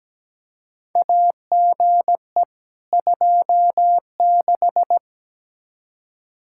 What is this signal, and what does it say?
Morse code, heard as a keyed tone, about 17 wpm, 700 Hz. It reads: AGE 26
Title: Practice QSO - Exchange 3 - More Info - 17wpm